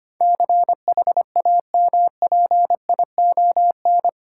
Morse code, heard as a keyed tone, about 25 wpm, 700 Hz. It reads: CHAMPION
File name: Eight Letter Words - 25wpm